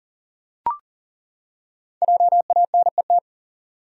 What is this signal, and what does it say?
Morse code, heard as a keyed tone, about 40 wpm, 700 Hz. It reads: JANET
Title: USA Common Names - Max 5 Letters - 40wpm